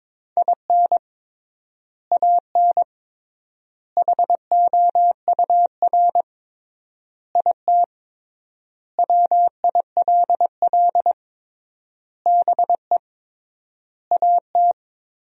Morse code, heard as a keyed tone, about 22 wpm, 700 Hz. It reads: IN AN HOUR IT WILL BE AT
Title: 3x Word-Spacing - Sentences from Top 400 Words - 22wpm